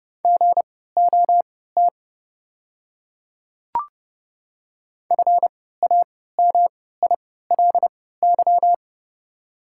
Morse code, heard as a keyed tone, about 30 wpm, spaced as for 17 wpm, 690 Hz. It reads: GOT FAMILY
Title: 3x Character Spacing - Top 500 Words - 30wpm